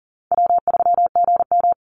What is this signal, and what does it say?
Morse code, heard as a keyed tone, about 40 wpm, 700 Hz. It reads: W3GM
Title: USA Call Signs 40wpm